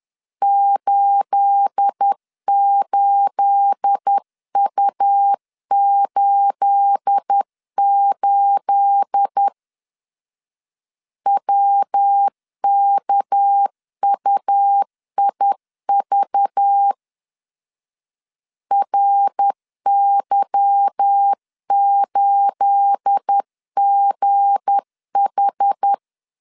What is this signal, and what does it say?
Morse code, heard as a keyed tone, about 10 wpm, 800 Hz. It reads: 88U88 WKUIV RY8GH
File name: LEZIONI / 9 28 10wpm 2842